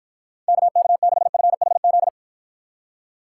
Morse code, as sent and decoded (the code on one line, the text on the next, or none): -.- -.- -.... ..-. .... --..
KK6FHZ